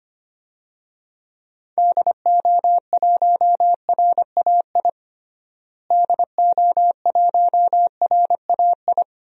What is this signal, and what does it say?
Morse code, heard as a keyed tone, about 25 wpm, 700 Hz. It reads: DO1RAI DO1RAI
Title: DXCC-100 Call Signs - 6 Characters - Repeated 2x - 25wpm